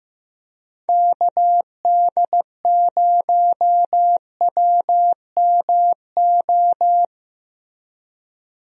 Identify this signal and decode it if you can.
Morse code, heard as a keyed tone, about 15 wpm, 700 Hz. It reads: KD0WMO
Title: USA Call Signs - 6 Characters - Repeated 3x - 15wpm